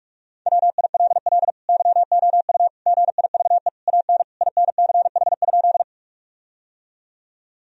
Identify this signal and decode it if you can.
Morse code, heard as a keyed tone, about 46 wpm, 700 Hz. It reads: WILL YOU GIVE AN INCH?
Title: Sentences from Top 1000 Words - Review Part 3 - 45wpm